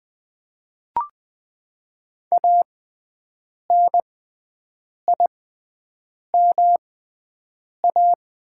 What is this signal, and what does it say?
Morse code, heard as a keyed tone, about 20 wpm, spaced as for 7 wpm, 700 Hz. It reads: ANIMA
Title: 6x Character Spacing - Top 300 Words - 20wpm